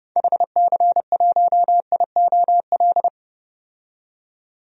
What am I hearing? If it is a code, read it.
Morse code, heard as a keyed tone, about 30 wpm, 700 Hz. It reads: HC1IOL
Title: DXCC-100 Call Signs - 6 Characters - 30wpm